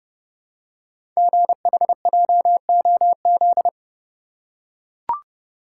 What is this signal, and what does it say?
Morse code, heard as a keyed tone, about 30 wpm, 690 Hz. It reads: GHJOZ
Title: Code Groups - 5 Letters - 30wpm